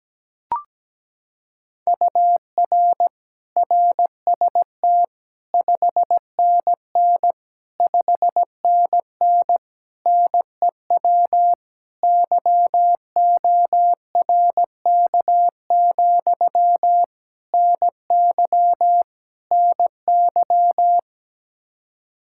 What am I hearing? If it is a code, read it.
Morse code, heard as a keyed tone, about 17 wpm, 700 Hz. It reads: UR RST 5NN 5NN NEW YORK, NY NY
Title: Practice QSO - Exchange 1 - Beginning - 17wpm